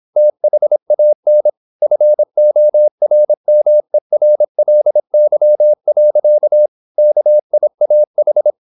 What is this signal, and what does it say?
Morse code, heard as a keyed tone, about 26 wpm, 600 Hz. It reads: THAN FORMERLY. KIAH
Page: Never 26wpm | CW med Gnister